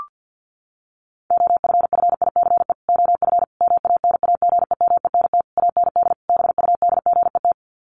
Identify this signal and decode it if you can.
Morse code, heard as a keyed tone, about 50 wpm, 700 Hz. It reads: OFFICE OF MANAGEMENT AND BUDGET
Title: Sets of 5 Words Encore2 50wpm